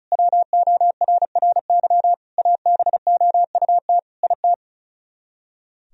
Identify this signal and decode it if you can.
Morse code, heard as a keyed tone, about 46 wpm, 700 Hz. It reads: WORRY ABOUT IT